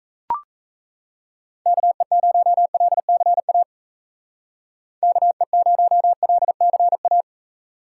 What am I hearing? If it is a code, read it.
Morse code, heard as a keyed tone, about 36 wpm, 700 Hz. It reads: KE0LCA KE0LCA